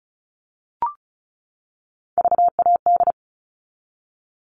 Morse code, heard as a keyed tone, about 34 wpm, 700 Hz. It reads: VAD
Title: All Cut Numbers - 3-Digit 35wpm